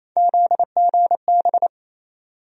Morse code, heard as a keyed tone, about 28 wpm, 700 Hz. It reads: ZGB
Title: USA Call Sign Postfixes - 3 Letters 28wpm